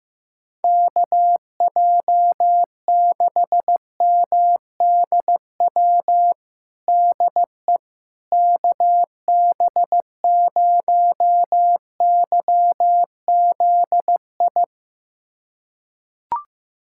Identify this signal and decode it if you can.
Morse code, heard as a keyed tone, about 16 wpm, 700 Hz. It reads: KJ6MDW DE KB0YZI
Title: USA Callsign DE Callsign 15wpm